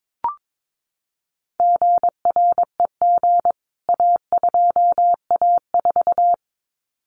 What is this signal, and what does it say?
Morse code, heard as a keyed tone, about 22 wpm, 700 Hz. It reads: GREG A2A4
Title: CWT Exchange 22wpm